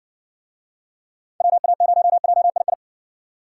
Morse code, heard as a keyed tone, about 60 wpm, 700 Hz. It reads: WA0JII